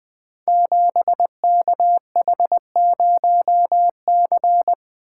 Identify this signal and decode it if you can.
Morse code, heard as a keyed tone, about 20 wpm, 700 Hz. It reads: 7KH0C
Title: Code Groups - 5 Letter:Numbers - 20wpm